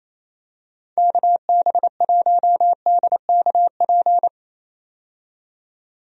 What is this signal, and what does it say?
Morse code, heard as a keyed tone, about 28 wpm, 700 Hz. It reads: KB1DKP
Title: USA Call Signs - 6 Characters - 28wpm